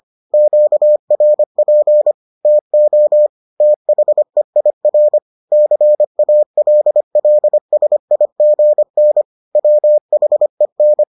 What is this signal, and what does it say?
Morse code, heard as a keyed tone, about 25 wpm, 600 Hz. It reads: QRP TO THEIR CALLSIGN WHEN